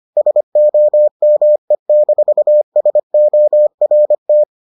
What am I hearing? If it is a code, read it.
Morse code, heard as a keyed tone, about 25 wpm, 600 Hz. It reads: SOME-SORT